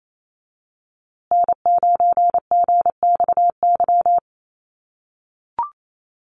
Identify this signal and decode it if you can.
Morse code, heard as a keyed tone, about 28 wpm, 700 Hz. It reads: N9GXY
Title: USA Call Signs Repeated 3x 28wpm